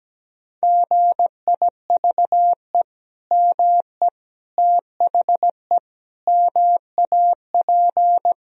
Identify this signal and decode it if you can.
Morse code, heard as a keyed tone, about 17 wpm, 700 Hz. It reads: GIVE ME THE MAP
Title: Sentences from Top 1000 Words - Review Part 2 - 17wpm